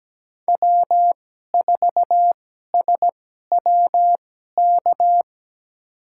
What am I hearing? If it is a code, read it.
Morse code, heard as a keyed tone, about 17 wpm, spaced as for 12 wpm, 700 Hz. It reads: W4SWK
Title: USA Call Signs - 2x Character Spacing - 17wpm